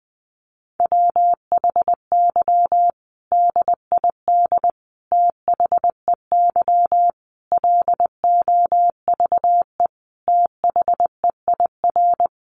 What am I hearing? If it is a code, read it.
Morse code, heard as a keyed tone, about 19 wpm, 700 Hz. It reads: WHY DID THEY LOVE THEIR
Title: Sentences from Top 1500 Words 20wpm